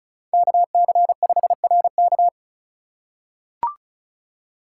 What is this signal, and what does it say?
Morse code, heard as a keyed tone, about 36 wpm, 700 Hz. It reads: KC5RK
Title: USA Call Signs - 5 Characters - 2x Character Spacing - 35wpm